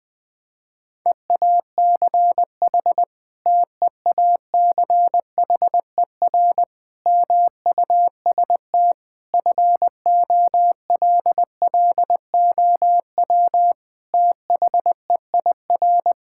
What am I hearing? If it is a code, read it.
Morse code, heard as a keyed tone, about 20 wpm, 700 Hz. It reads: EACH TEACHER MUST FOLLOW THEIR
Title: Sentences from Top 1000 Words - Review Part 6 - 20wpm